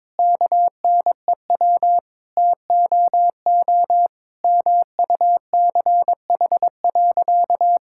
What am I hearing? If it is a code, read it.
Morse code, heard as a keyed tone, about 22 wpm, 700 Hz. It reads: KNEW TOO MUCH.
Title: Sentences from Top 1000 Words - Review Part 5 - 22wpm